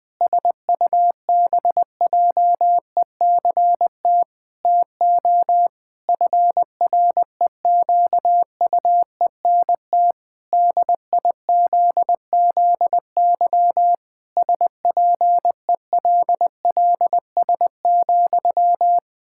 Morse code, heard as a keyed tone, about 20 wpm, 700 Hz. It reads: SUBJECT TO FREQUENT DIZZY SPELLS,
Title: Pangram Challenge v2 - 20wpm